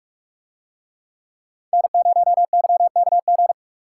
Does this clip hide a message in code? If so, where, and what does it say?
Yes: Morse code from 1.7 to 3.5 s, heard as a keyed tone, about 45 wpm, 700 Hz. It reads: N0YKG